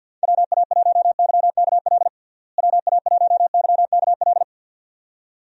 Morse code, heard as a keyed tone, about 46 wpm, 700 Hz. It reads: WA1YKL WA1YKL
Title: USA Call Signs - 6 Characters - Repeated 2x - 50wpm